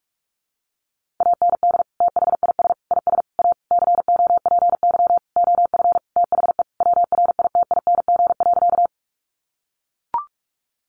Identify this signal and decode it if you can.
Morse code, heard as a keyed tone, about 44 wpm, 700 Hz. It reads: AND THIS IS A COPY OF THE WRITING.
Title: Sentences from Top 1400 Words 45wpm